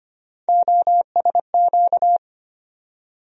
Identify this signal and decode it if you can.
Morse code, heard as a keyed tone, about 25 wpm, 700 Hz. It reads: OSQ
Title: USA Call Sign Postfixes - 3 Letters 25wpm